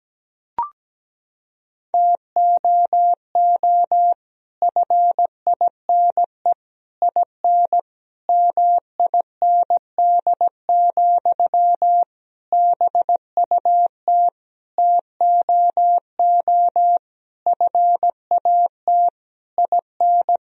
Morse code, heard as a keyed tone, about 17 wpm, 700 Hz. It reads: TOO FINE IN MIND, BUT TOO FAT IN
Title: Sentences from Top 1000 Words - Review Part 4 - 17wpm